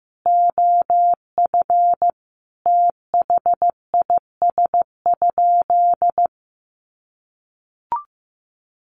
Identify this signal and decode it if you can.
Morse code, heard as a keyed tone, about 15 wpm, 700 Hz. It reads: OF THIS?
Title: Sentences from Top 900 Words 15wpm